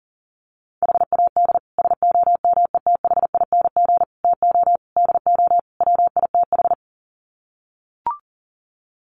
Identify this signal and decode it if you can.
Morse code, heard as a keyed tone, about 40 wpm, 700 Hz. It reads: HAD SOMETHING TO DO WITH
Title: Sets of 5 Words 40wpm